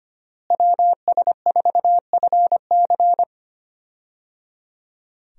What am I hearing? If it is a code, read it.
Morse code, heard as a keyed tone, about 25 wpm, 700 Hz. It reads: WS4FC
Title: USA Call Signs - 5 Characters - Repeated 3x - 25wpm